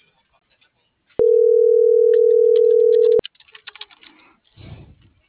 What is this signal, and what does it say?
Ringback tone (North America), 440 + 480 Hz.